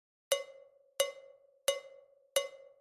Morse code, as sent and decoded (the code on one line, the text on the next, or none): none